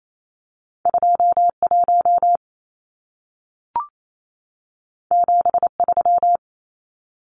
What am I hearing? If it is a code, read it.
Morse code, heard as a keyed tone, about 28 wpm, 700 Hz. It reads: 21 73
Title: Numbers - 2-Digit 28wpm